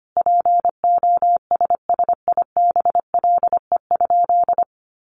Morse code, heard as a keyed tone, about 25 wpm, 700 Hz. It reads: POSSIBLE?
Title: Sentences from Top 500 Words 25wpm